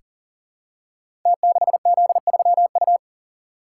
Morse code, heard as a keyed tone, about 42 wpm, 700 Hz. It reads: T6Z3U
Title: Code Groups - 5 Letter:Numbers - 40wpm